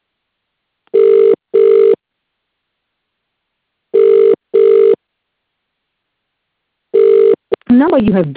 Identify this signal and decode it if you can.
Ringback tone (UK), 400 + 450 Hz, with speech.